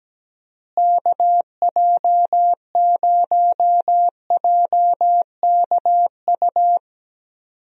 Morse code, heard as a keyed tone, about 17 wpm, 700 Hz. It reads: KJ0JKU
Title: USA Call Signs - 6 Characters - Repeated 2x - 17wpm